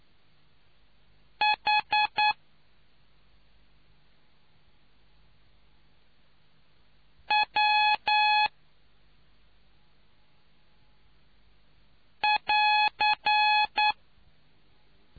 Morse code, heard as a keyed tone, about 9 wpm, 840 Hz. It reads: H W <AR>